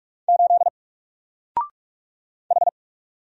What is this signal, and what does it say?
Morse code, heard as a keyed tone, about 46 wpm, 700 Hz. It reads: 8 H
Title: Single Character (Letter:Number:Punctuation:Prosign) 45wpm